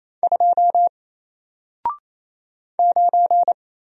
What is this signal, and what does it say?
Morse code, heard as a keyed tone, about 27 wpm, 700 Hz. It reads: 2 9
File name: Single-Letter-Number-28wpm